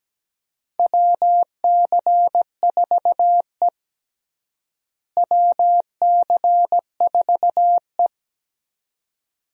Morse code, heard as a keyed tone, about 17 wpm, 700 Hz. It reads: WC4E WC4E
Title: USA Call Signs - 4 Characters - Repeated 3x - 17wpm